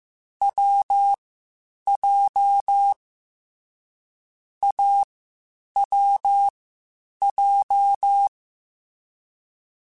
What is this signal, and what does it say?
Morse code, heard as a keyed tone, about 15 wpm, spaced as for 8 wpm, 800 Hz. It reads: WJ AWJ